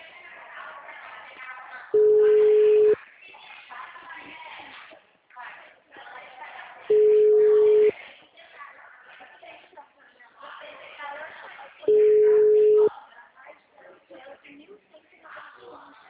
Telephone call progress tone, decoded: ringback tone

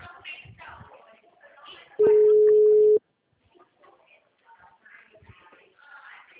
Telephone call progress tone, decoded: ringback tone

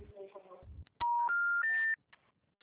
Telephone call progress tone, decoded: special information tone

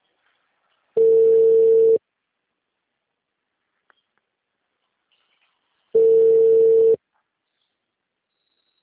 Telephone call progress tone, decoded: ringback tone

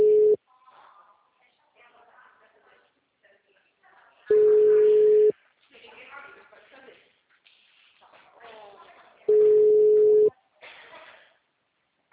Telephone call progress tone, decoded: ringback tone

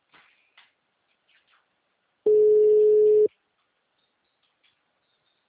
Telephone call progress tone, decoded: ringback tone

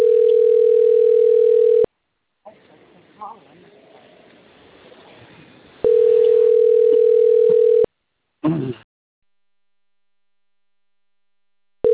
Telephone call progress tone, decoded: ringback tone

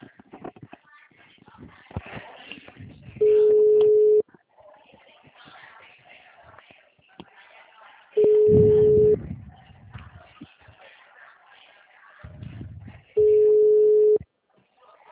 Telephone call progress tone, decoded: ringback tone

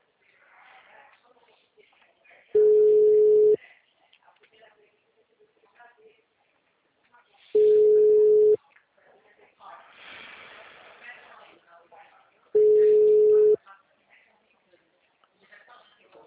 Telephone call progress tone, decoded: ringback tone

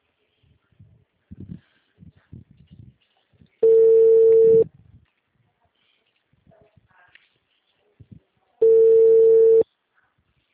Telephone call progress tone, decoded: ringback tone

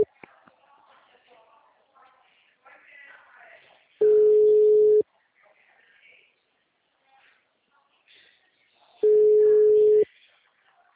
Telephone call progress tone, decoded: ringback tone

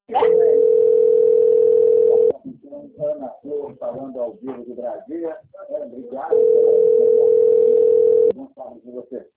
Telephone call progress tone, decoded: ringback tone

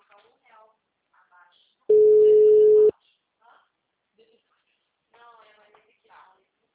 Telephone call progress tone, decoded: ringback tone

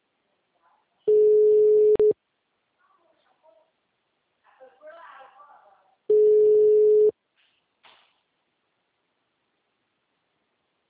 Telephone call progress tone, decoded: ringback tone

